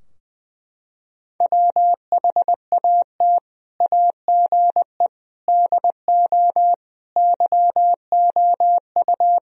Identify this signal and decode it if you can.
Morse code, heard as a keyed tone, about 20 wpm, 700 Hz. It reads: WHAT AGE DO YOU
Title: Sentences from Top 1000 Words - Review Part 3 - 20wpm